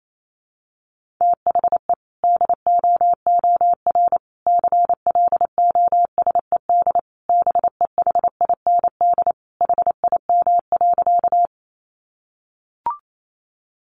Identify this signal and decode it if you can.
Morse code, heard as a keyed tone, about 28 wpm, 700 Hz. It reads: THE DOOR CLOSED BEHIND HIM.
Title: Sentences from Top 1400 Words 28wpm